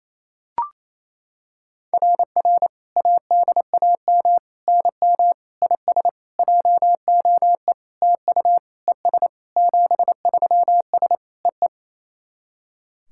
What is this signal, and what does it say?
Morse code, heard as a keyed tone, about 28 wpm, 700 Hz. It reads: RR ADAM NM IS JOE TU ES 73S EE
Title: SOTA QSO Practice 28wpm